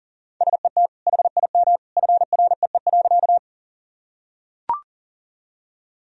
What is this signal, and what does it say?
Morse code, heard as a keyed tone, about 40 wpm, 700 Hz. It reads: SET HIM FREE.
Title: Sentences from Top 1000 Words - Review Part 1 - 40wpm